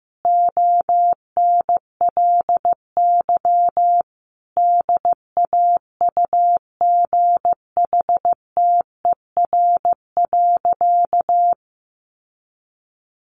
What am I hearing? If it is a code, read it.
Morse code, heard as a keyed tone, about 15 wpm, 700 Hz. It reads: ONLY DAUGHTER.
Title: Sentences from Top 1500 Words 15wpm